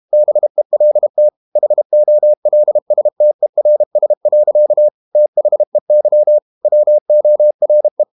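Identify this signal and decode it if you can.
Morse code, heard as a keyed tone, about 32 wpm, 600 Hz. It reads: BELT HOLSTERS. THEY WORE
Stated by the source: Morse code practice at 32 wpm from CW med Gnister, reading Ken Follett's novel Never in English.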